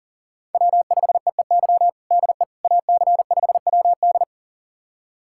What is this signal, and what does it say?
Morse code, heard as a keyed tone, about 37 wpm, 700 Hz. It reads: W5EEY DE AC5WD
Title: USA Callsign DE Callsign - 5 Characters - 40wpm